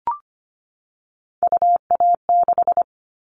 Morse code, heard as a keyed tone, about 25 wpm, 700 Hz. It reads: UA6
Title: Callsign Prefixes for Top 50 DXCC - 25wpm